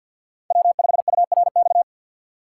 Morse code, heard as a keyed tone, about 53 wpm, 700 Hz. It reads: W5URX
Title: USA Call Signs - 5 Characters - 4x Character Spacing - 50wpm